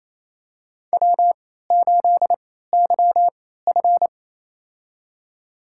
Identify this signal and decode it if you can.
Morse code, heard as a keyed tone, about 27 wpm, spaced as for 16 wpm, 700 Hz. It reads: W8YF
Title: USA Call Signs - 4 Characters - 3x Character Spacing - 28wpm